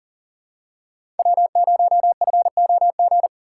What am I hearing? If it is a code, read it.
Morse code, heard as a keyed tone, about 40 wpm, 700 Hz. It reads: W0FOG
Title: USA Call Signs - 5 Characters - 2x Character Spacing - 40wpm